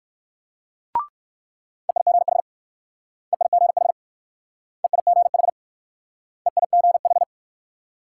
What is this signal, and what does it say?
Morse code, heard as a keyed tone, about 65 wpm, 700 Hz. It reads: EIGH EIGH EIGH EIGH